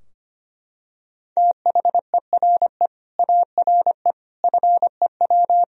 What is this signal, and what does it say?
Morse code, heard as a keyed tone, about 25 wpm, 700 Hz. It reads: THERE ARE FEW
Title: Sentences from Top 1000 Words - Review Part 4 - 25wpm